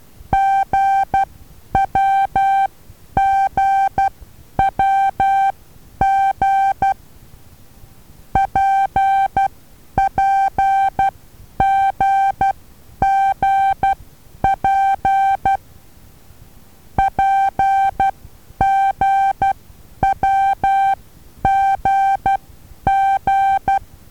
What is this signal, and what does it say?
Morse code, heard as a keyed tone, about 12 wpm, spaced as for 9 wpm, 780 Hz. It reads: GWGWG PPGGP PGWGG
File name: Taviro HA6VA B